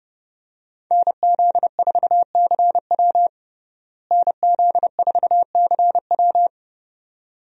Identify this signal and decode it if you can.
Morse code, heard as a keyed tone, about 30 wpm, 700 Hz. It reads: NZ4CW NZ4CW
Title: USA Call Signs - 5 Characters - Repeated 2x - 30wpm